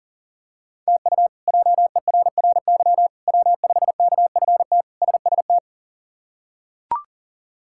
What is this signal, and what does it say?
Morse code, heard as a keyed tone, about 40 wpm, 700 Hz. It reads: TU JERRY W5KFT SST
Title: K1USN SST QSO Practice - 40wpm